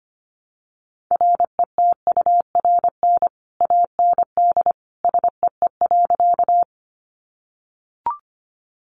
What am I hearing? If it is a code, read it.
Morse code, heard as a keyed tone, about 25 wpm, 700 Hz. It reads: RETURN AND SEE.
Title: Sentences from Top 800 Words 25wpm